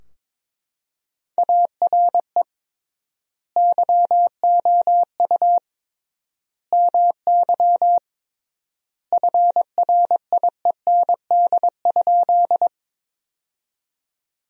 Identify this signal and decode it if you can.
Morse code, heard as a keyed tone, about 22 wpm, 700 Hz. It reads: ARE YOU MY FRIEND?
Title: 3x Word-Spacing - Sentences from Top 300 Words - 22wpm